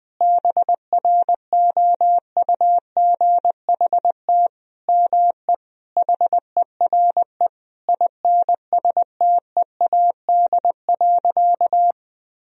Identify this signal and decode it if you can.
Morse code, heard as a keyed tone, about 21 wpm, 700 Hz. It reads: BROUGHT ME HERE INSTEAD.
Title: Sentences from Top 1000 Words - Review Part 4 - 20wpm